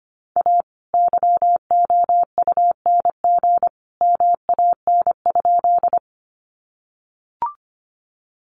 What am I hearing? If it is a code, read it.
Morse code, heard as a keyed tone, about 25 wpm, 700 Hz. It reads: A YOUNG MAN?
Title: Sentences from Top 400 Words 25wpm